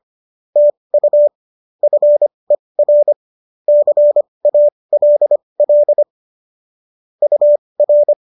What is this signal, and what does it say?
Morse code, heard as a keyed tone, about 25 wpm, spaced as for 20 wpm, 590 Hz. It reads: TU FER CALL UR